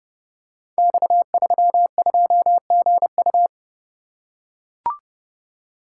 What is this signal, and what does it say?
Morse code, heard as a keyed tone, about 30 wpm, 700 Hz. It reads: X32GU
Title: Code Groups - 5 Letter:Numbers - 30wpm